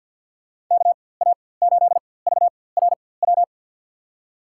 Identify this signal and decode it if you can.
Morse code, heard as a keyed tone, about 49 wpm, spaced as for 23 wpm, 700 Hz. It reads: KA8VRW
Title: USA Call Signs - 6 Characters - 4x Character Spacing - 50wpm